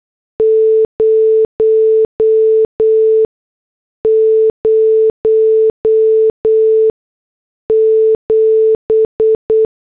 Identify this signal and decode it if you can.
Morse code, heard as a keyed tone, about 8 wpm, spaced as for 6 wpm, 440 Hz. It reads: 007
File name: morse / headlines